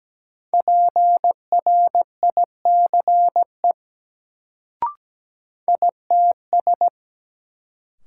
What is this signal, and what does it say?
Morse code, heard as a keyed tone, about 17 wpm, 700 Hz. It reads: PRICE ITS
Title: Sentences from Top 600 Words - New Word Drill - 17wpm